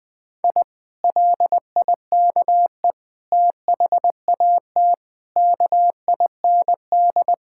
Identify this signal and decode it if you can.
Morse code, heard as a keyed tone, about 20 wpm, 700 Hz. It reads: I LIKE THAT KIND
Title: Sentences with 3-Word Phrases - 20wpm